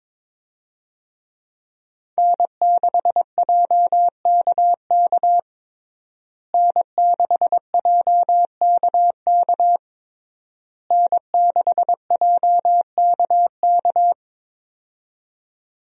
Morse code, heard as a keyed tone, about 22 wpm, 700 Hz. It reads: N6JKK N6JKK N6JKK
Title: USA Call Signs - 5 Characters - Repeated 3x - 22wpm